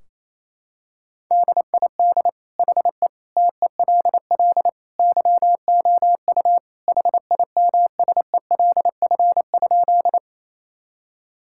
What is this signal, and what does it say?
Morse code, heard as a keyed tone, about 27 wpm, 700 Hz. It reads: DID HE TELL YOU HIMSELF?
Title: Sentences from Top 1600 Words 28wpm